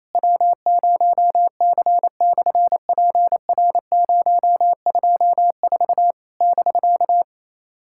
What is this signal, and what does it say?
Morse code, heard as a keyed tone, about 28 wpm, 700 Hz. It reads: W0C/PR024 <BK>
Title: SOTA QSO Practice 28wpm